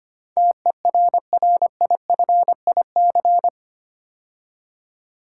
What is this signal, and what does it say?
Morse code, heard as a keyed tone, about 25 wpm, 700 Hz. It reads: TERRIFIC
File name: 8 Letter Words - First 3 Letters Given - 25wpm